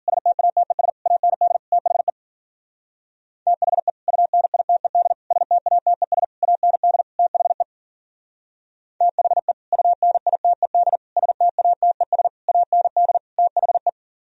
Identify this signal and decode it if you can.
Morse code, heard as a keyed tone, about 46 wpm, 700 Hz. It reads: STATES AND THE THE UNITED STATES AND THE THE UNITED STATES AND THE
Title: Speed-Racing - Sets of 5 Words 40wpm